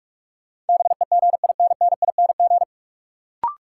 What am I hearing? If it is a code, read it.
Morse code, heard as a keyed tone, about 46 wpm, 700 Hz. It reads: BEGINNING
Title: Sentences from Top 800 Words - New Word Drill - 45wpm